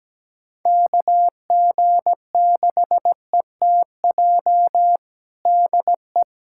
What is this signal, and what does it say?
Morse code, heard as a keyed tone, about 17 wpm, 700 Hz. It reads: KG6ETJ DE
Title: USA Callsign DE Callsign 17wpm